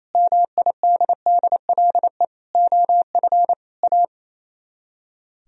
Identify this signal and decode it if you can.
Morse code, heard as a keyed tone, about 33 wpm, 700 Hz. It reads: MIDDLE OF A
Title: Speed-Racing - Sets of 5 Words 28wpm